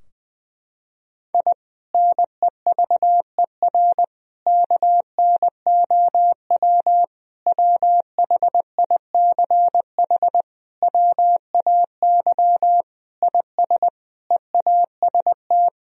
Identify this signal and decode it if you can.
Morse code, heard as a keyed tone, about 21 wpm, 700 Hz. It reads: I NEVER KNOW WHICH WAY IS EAST
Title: Sentences from Top 1000 Words - Review Part 3 - 20wpm